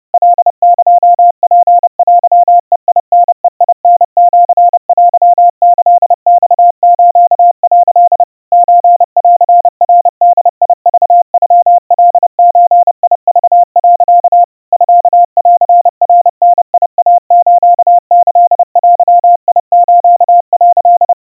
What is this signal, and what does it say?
Morse code (Japanese Wabun code), heard as a keyed tone, about 30 wpm, 700 Hz. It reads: カエツテベタベタシテキマス」 ソンナボクノカゾク、 ミンナダイスキデス」
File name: kawa / Morse sample / kamyu 110 cpm 700 Hz Crispy